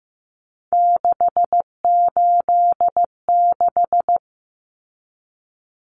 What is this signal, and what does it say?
Morse code, heard as a keyed tone, about 15 wpm, 700 Hz. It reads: 686
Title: Common Cut Numbers - 3-Digit 15wpm